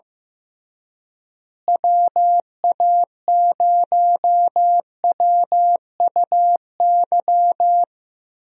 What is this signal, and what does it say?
Morse code, heard as a keyed tone, about 15 wpm, 700 Hz. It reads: WA0WUY
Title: USA Call Signs - 6 Characters - Repeated 3x - 15wpm